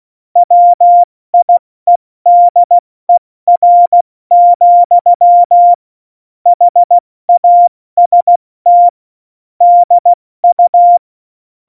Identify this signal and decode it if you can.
Morse code, heard as a keyed tone, about 16 wpm, 700 Hz. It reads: WIEDER, HAST DU
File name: Morsekurs / Witze / Witz 14 April 2020, 70 F80